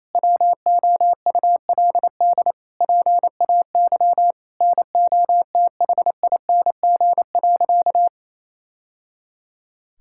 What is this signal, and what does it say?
Morse code, heard as a keyed tone, about 28 wpm, 700 Hz. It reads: WOULD PAY NOTHING.
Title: Sentences from Top 1000 Words - Review Part 5 - 28wpm